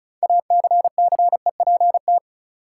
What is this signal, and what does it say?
Morse code, heard as a keyed tone, about 36 wpm, 700 Hz. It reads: ACCEPT
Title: Sentences from Top 1300 Words - New Word Drill - 35wpm